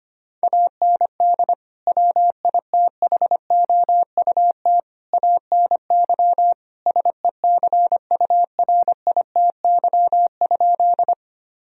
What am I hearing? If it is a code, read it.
Morse code, heard as a keyed tone, about 25 wpm, 700 Hz. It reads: AND WITHOUT ANY SECURITY?
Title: Sentences from Top 1000 Words - Review Part 4 - 25wpm